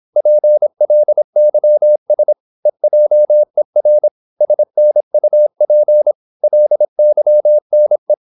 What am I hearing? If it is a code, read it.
Morse code, heard as a keyed tone, about 26 wpm, 600 Hz. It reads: PLYS EJER SNUP LYNE
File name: Korte ord DK 26wpm